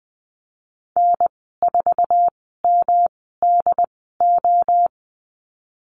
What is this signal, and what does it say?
Morse code, heard as a keyed tone, about 19 wpm, spaced as for 14 wpm, 700 Hz. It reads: N4MDO
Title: 173 Lesson 28g - Number 9 - Call Signs 20wpm